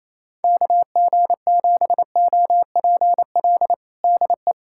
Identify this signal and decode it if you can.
Morse code, heard as a keyed tone, about 28 wpm, 700 Hz. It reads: KG7OPL DE
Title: USA Callsign DE Callsign 28wpm